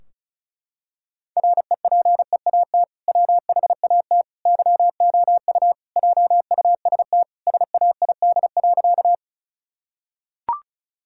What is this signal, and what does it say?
Morse code, heard as a keyed tone, about 36 wpm, 700 Hz. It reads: REPEAT WHAT YOU JUST SAID.
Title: Sentences from Top 1600 Words 35wpm